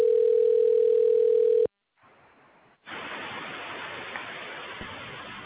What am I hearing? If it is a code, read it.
Ringback tone (North America), 440 + 480 Hz.